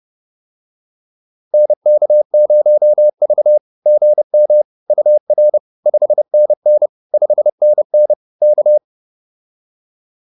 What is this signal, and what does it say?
Morse code, heard as a keyed tone, about 30 wpm, 600 Hz. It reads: NK0V GM UR 5NN 5NN K
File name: POTA101 30